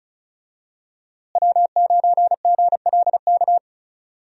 Morse code, heard as a keyed tone, about 36 wpm, 700 Hz. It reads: W9GLK
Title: USA Call Signs - 5 Characters - Repeated 2x - 35wpm